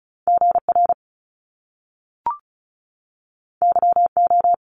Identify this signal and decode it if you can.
Morse code, heard as a keyed tone, about 36 wpm, 700 Hz. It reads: G R Y O
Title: Top Two Letter Combinations in Words 35wpm